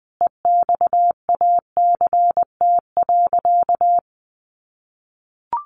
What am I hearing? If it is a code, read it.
Morse code, heard as a keyed tone, about 20 wpm, 700 Hz. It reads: EXACT.
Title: Sentences from Top 1300 Words 20wpm